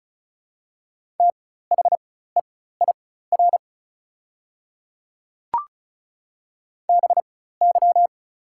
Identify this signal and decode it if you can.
Morse code, heard as a keyed tone, about 36 wpm, spaced as for 16 wpm, 700 Hz. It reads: THEIR BY
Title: 4x Character Spacing - Top 200 Words - 35wpm